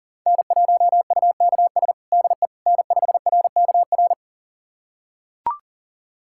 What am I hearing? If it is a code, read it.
Morse code, heard as a keyed tone, about 40 wpm, 700 Hz. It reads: N1UKS DE N5RKR
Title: USA Callsign DE Callsign - 5 Characters - 40wpm